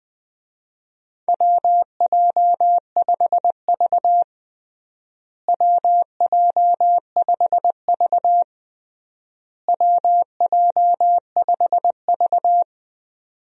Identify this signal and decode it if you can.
Morse code, heard as a keyed tone, about 20 wpm, 700 Hz. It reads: WJ5V WJ5V WJ5V
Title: USA Call Signs - 4 Characters - Repeated 3x - 20wpm